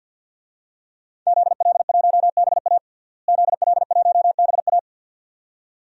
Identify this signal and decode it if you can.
Morse code, heard as a keyed tone, about 47 wpm, 700 Hz. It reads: ZL1BA ZL1BA
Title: DXCC-100 Call Signs - 5 Characters - Repeated 2x - 50wpm